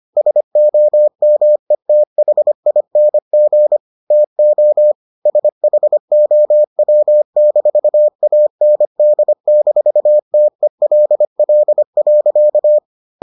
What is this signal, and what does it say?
Morse code, heard as a keyed tone, about 25 wpm, 600 Hz. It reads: SOMETHING TO SHOW-AND-TELL.